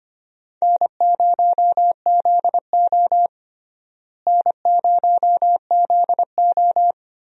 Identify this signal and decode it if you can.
Morse code, heard as a keyed tone, about 25 wpm, 700 Hz. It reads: N0ZO N0ZO
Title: USA Call Signs - 4 Characters - Repeated 3x - 25wpm